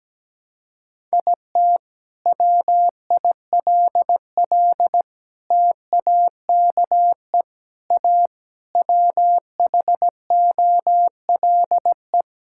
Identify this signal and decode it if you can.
Morse code, heard as a keyed tone, about 17 wpm, 700 Hz. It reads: IT WILL TAKE A WHOLE
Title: Sentences from Top 1000 Words - Review Part 5 - 17wpm